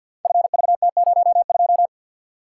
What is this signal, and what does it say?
Morse code, heard as a keyed tone, about 48 wpm, 700 Hz. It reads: FVT02
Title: Code Groups - 5 Letter:Numbers - 50wpm